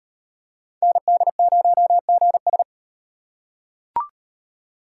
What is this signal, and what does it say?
Morse code, heard as a keyed tone, about 38 wpm, 700 Hz. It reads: ND0GS